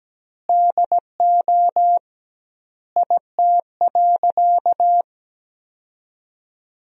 Morse code, heard as a keyed tone, about 17 wpm, 700 Hz. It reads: DO IT.
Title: 2x Word-Spacing - Sentences from Top 100 Words - 17wpm